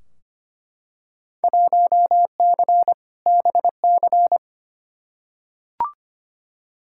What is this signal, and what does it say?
Morse code, heard as a keyed tone, about 25 wpm, 700 Hz. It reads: 1C BC
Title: Field Day Exchanges Encore - 25wpm